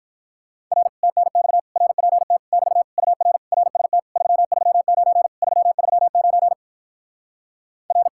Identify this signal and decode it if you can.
Morse code, heard as a keyed tone, about 46 wpm, 700 Hz. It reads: R TNX RPT <BT> UR RST 339 339 R